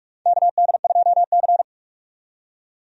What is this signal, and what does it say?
Morse code, heard as a keyed tone, about 41 wpm, 700 Hz. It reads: KD2C
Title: USA Call Signs - 4 Characters - 45wpm